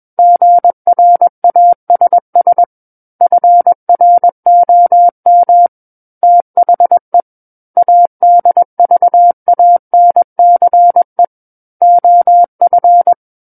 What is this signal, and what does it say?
Morse code, heard as a keyed tone, about 21 wpm, 700 Hz. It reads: GRASS FROM THE ADVANCE OF
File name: Book One / 12-Chapter 12 - 21 WPM